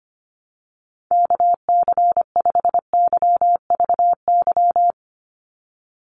Morse code, heard as a keyed tone, about 25 wpm, 700 Hz. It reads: KC5YVY
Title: USA Call Signs 25wpm